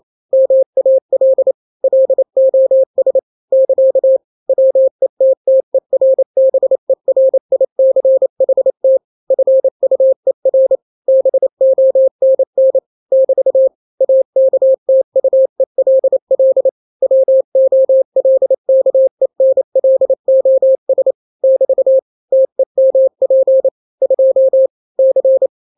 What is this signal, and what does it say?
Morse code, heard as a keyed tone, about 28 wpm, 530 Hz. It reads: MAL LOS <KA> WETTERBERICHT FUER BONN <BT> AKTUELL WOLKENLOS <BT> TEMP 2 C
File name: daily / 27wpm